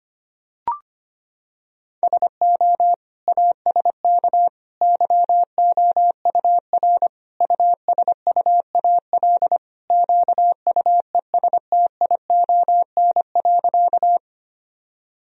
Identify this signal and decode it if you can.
Morse code, heard as a keyed tone, about 25 wpm, 700 Hz. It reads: SO ASK YOUR USUAL QUESTION.
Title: Sentences from Top 1000 Words - Review Part 2 - 25wpm